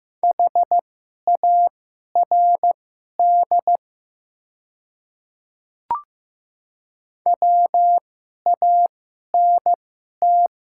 Morse code, heard as a keyed tone, about 15 wpm, spaced as for 11 wpm, 700 Hz. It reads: HARD WANT
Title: 2x Character Spacing - Top 300 Words - 15wpm